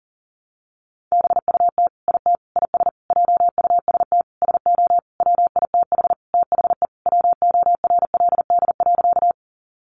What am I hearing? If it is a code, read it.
Morse code, heard as a keyed tone, about 40 wpm, 700 Hz. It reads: BUT IT IS JUST SO WITH THE WORLD.
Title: Sentences from Top 200 Words 40wpm